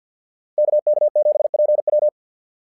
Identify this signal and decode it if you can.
Morse code, heard as a keyed tone, about 49 wpm, 600 Hz. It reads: KK7PW
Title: SOTA QSO Practice 50wpm